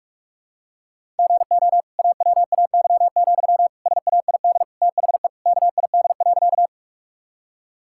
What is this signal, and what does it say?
Morse code, heard as a keyed tone, about 46 wpm, 700 Hz. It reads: GO AWAY, SAID THE KID.
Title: Sentences from Top 1600 Words 45wpm